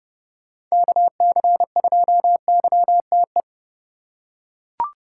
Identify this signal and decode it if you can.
Morse code, heard as a keyed tone, about 30 wpm, 700 Hz. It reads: KC2YTE
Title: USA Call Signs - 6 Characters - 3x Character Spacing - 30wpm